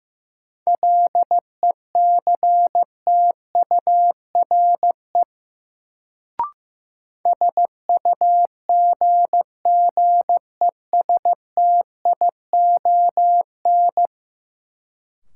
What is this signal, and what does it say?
Morse code, heard as a keyed tone, about 16 wpm, 700 Hz. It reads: LECTURE SUGGESTION
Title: Sentences from Top 1600 Words - New Word Drill - 15wpm